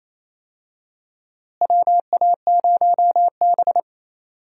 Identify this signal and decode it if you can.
Morse code, heard as a keyed tone, about 27 wpm, 700 Hz. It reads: WA0B
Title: USA Call Signs - 4 Characters - Repeated 3x - 28wpm